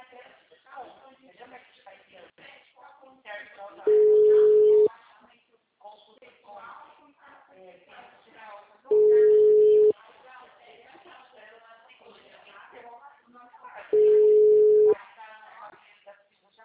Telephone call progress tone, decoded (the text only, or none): ringback tone